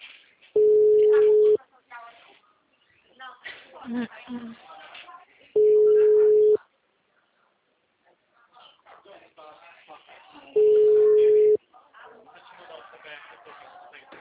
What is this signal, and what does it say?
Ringback tone (Europe, 425 Hz), with speech.